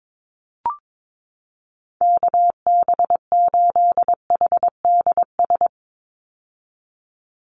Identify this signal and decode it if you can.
Morse code, heard as a keyed tone, about 22 wpm, 700 Hz. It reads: KB8HDS
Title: USA Call Signs 22wpm